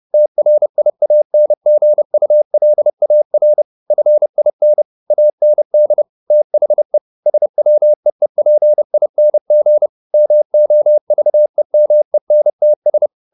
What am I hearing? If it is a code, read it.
Morse code, heard as a keyed tone, about 30 wpm, 600 Hz. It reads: TRIANGULAR FIN AND THE SWEEPING MOVEMENTS